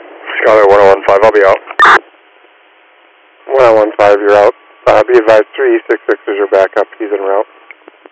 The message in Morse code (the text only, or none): none